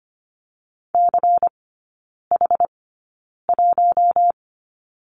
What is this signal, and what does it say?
Morse code, heard as a keyed tone, about 25 wpm, spaced as for 9 wpm, 700 Hz. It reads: CH1
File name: course / 25:10wpm / 093 Lesson 16e - Letter c - Three Characters 25wpm